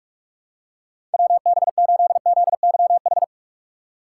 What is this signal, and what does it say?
Morse code, heard as a keyed tone, about 46 wpm, 700 Hz. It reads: WB8ZYH
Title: USA Call Signs - 2x Character Spacing - 45wpm